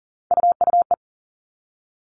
Morse code, heard as a keyed tone, about 40 wpm, 700 Hz. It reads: UUE
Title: All Cut Numbers - 3-Digit 40wpm